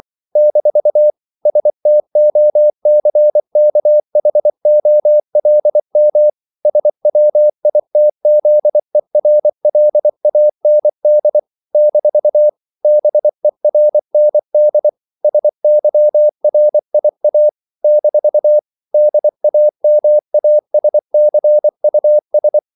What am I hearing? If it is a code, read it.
Morse code, heard as a keyed tone, about 24 wpm, 600 Hz. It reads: - STOCKHOLM SWITZERLAND - BERND SYRIA - DAMASCUS
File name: CW / Practice / NationalCapitals24